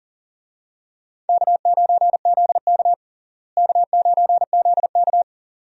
Morse code, heard as a keyed tone, about 37 wpm, 700 Hz. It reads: K9ZK K9ZK
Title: USA Call Signs - 4 Characters - Repeated 2x - 40wpm